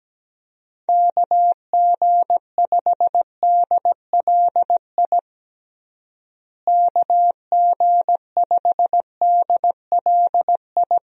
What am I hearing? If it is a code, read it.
Morse code, heard as a keyed tone, about 17 wpm, 700 Hz. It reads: KG5DLI KG5DLI
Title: USA Call Signs - 6 Characters - Repeated 3x - 17wpm